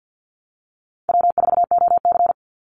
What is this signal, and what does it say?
Morse code, heard as a keyed tone, about 50 wpm, 700 Hz. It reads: W4OC
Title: Easy Call Signs - 50wpm